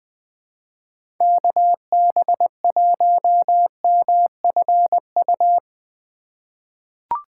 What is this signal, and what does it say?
Morse code, heard as a keyed tone, about 20 wpm, 700 Hz. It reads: KB1MFU
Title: USA Call Signs - 3x Character Spacing - 20wpm